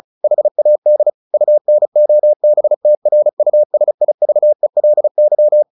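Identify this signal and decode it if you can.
Morse code, heard as a keyed tone, about 36 wpm, 600 Hz. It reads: HAD UNOBTRUSIVELY
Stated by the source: Morse practice at 35 wpm from CW med Gnister, sending Ken Follett's novel Never in English.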